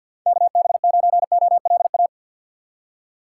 Morse code, heard as a keyed tone, about 52 wpm, 700 Hz. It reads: KB9OLA
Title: USA Call Signs - 2x Character Spacing - 50wpm